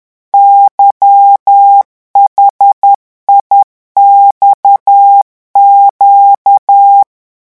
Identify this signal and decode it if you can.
Morse code, heard as a keyed tone, about 10 wpm, 800 Hz. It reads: YHIXQ
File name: morse mp3 / revision8-vitesse 10 mots